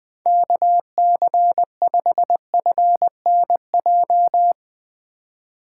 Morse code, heard as a keyed tone, about 20 wpm, 700 Hz. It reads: KC5FNJ
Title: USA Call Signs - 2x Character Spacing - 20wpm